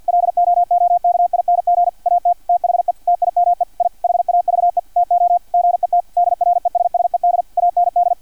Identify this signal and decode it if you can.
Morse code, heard as a keyed tone, about 50 wpm, 700 Hz, with radio noise. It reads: LOOKING AT THE TIME I HAVE TO GET DRESSED AND
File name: archived / CRSnet